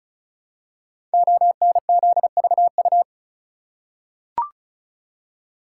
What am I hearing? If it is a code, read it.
Morse code, heard as a keyed tone, about 36 wpm, 690 Hz. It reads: ONZVU
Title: Code Groups - 5 Letter:Numbers - 35wpm